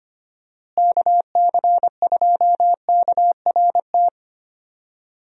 Morse code, heard as a keyed tone, about 25 wpm, 700 Hz. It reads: KC2KRT
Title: USA Call Signs - 6 Characters - Repeated 2x - 25wpm